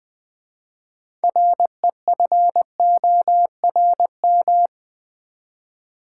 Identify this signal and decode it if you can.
Morse code, heard as a keyed tone, about 20 wpm, 700 Hz. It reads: REFORM
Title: 6 Letter Words - First 3 Letters Given - 20wpm